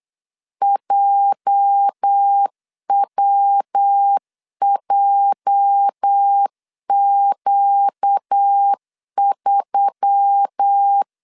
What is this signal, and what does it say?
Morse code, heard as a keyed tone, about 8 wpm, 800 Hz. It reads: JWJQ3
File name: LEZIONI / 8 YJ45 / 8 YJ 8wpm 3313